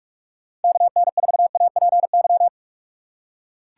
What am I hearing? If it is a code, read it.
Morse code, heard as a keyed tone, about 42 wpm, 700 Hz. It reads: KN4APY
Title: USA Call Signs - 6 Characters - 4x Character Spacing - 45wpm